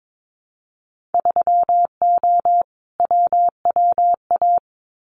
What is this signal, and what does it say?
Morse code, heard as a keyed tone, about 22 wpm, 700 Hz. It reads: 3O WWA
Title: Winter Field Day Exchanges Encore 22wpm